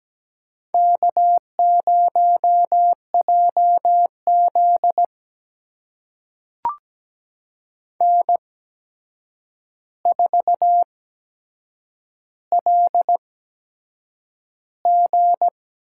Morse code, heard as a keyed tone, about 17 wpm, spaced as for 7 wpm, 700 Hz. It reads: K0JZ N4LG
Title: USA Call Signs - 4 Characters - 8x Character Spacing - 17wpm